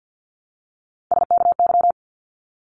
Morse code, heard as a keyed tone, about 50 wpm, 700 Hz. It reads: SKY
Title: Multi-letter Phonemes 50wpm